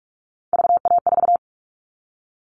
Morse code, heard as a keyed tone, about 44 wpm, 700 Hz. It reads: VA4
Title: All Cut Numbers - 3-Digit 45wpm